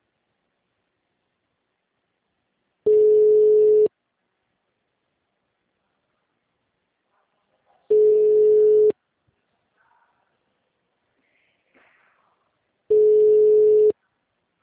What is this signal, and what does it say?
Ringback tone (Europe, 425 Hz).
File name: out- -2012- - - .45156